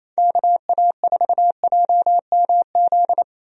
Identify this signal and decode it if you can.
Morse code, heard as a keyed tone, about 27 wpm, 700 Hz. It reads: KA4JMZ
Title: USA Call Signs - 6 Characters - 28wpm